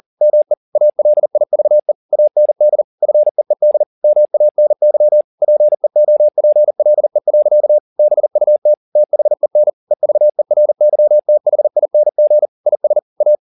Morse code, heard as a keyed tone, about 40 wpm, 600 Hz. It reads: ME ALIVE AND FEED MANY PEOPLE. BUT THEN EVERYTHING IS A
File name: the old man and the sea 2 40wpm